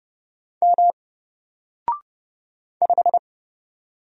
Morse code, heard as a keyed tone, about 30 wpm, 700 Hz. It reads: M 5
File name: Single-Letter-Number-30wpm